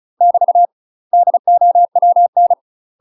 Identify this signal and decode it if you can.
Morse code, heard as a keyed tone, about 35 wpm, 700 Hz. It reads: <BT> DOWN
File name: UK MASTER QSOs TRAINING / 35WPM / UK MASTER QSOs 0016